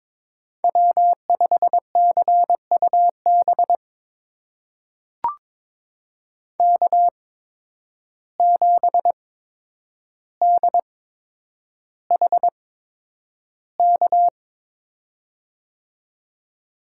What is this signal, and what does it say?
Morse code, heard as a keyed tone, about 22 wpm, spaced as for 9 wpm, 700 Hz. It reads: W5CUB K7DHK
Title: USA Call Signs - 5 Characters - 8x Character Spacing - 22wpm